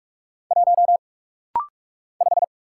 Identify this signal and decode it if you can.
Morse code, heard as a keyed tone, about 46 wpm, 700 Hz. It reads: 1 5
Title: Single Letter-Number - Rapid-Fire 45wpm